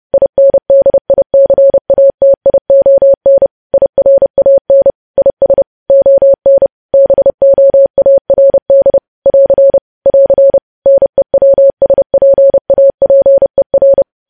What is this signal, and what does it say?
Morse code, heard as a keyed tone, about 30 wpm, 550 Hz. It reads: INDICATION IRAN IS ON BOARD <AR> <AR> NEWSPAPER